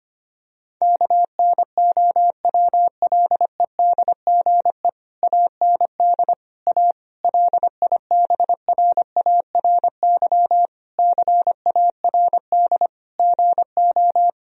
Morse code, heard as a keyed tone, about 25 wpm, 700 Hz. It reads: KNOWLEDGE AND A LIBRARY CARD GO
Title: Sentences from Top 1000 Words - Review Part 4 - 25wpm